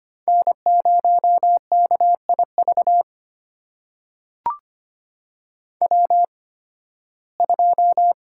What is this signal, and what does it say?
Morse code, heard as a keyed tone, about 25 wpm, spaced as for 16 wpm, 700 Hz. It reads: N0KIV W2
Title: USA Call Signs - 5 Characters - 8x Character Spacing - 25wpm